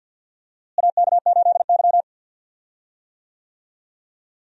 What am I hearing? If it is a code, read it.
Morse code, heard as a keyed tone, about 50 wpm, 700 Hz. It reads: AK8Y
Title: USA Call Signs - 4 Characters - Repeated 2x - 50wpm